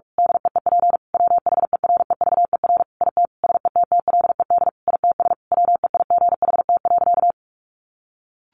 Morse code, heard as a keyed tone, about 44 wpm, 700 Hz. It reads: DEEP WHEREVER IT SETTLED ITS WEIGHT.
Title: A Sound Of Thunder 45wpm